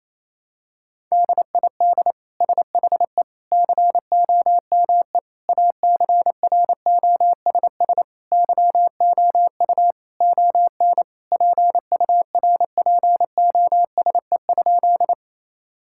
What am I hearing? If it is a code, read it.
Morse code, heard as a keyed tone, about 28 wpm, 700 Hz. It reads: DID SHE COME ACROSS YOU ON PURPOSE?
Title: Sentences from Top 1000 Words - Review Part 5 - 28wpm